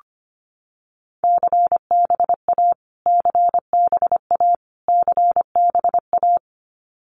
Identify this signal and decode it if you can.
Morse code, heard as a keyed tone, about 25 wpm, 700 Hz. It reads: CBA CBA CBA
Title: Abbreviations Repeated 3x - 25wpm